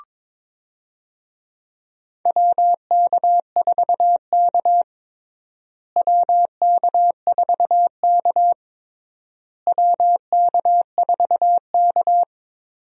Morse code, heard as a keyed tone, about 22 wpm, 700 Hz. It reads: WK4K WK4K WK4K
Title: USA Call Signs - 4 Characters - Repeated 3x - 22wpm